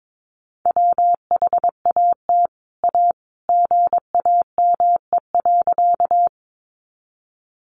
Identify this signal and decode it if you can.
Morse code, heard as a keyed tone, about 22 wpm, 700 Hz. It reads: WHAT A GAME.
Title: Sentences from Top 500 Words 22wpm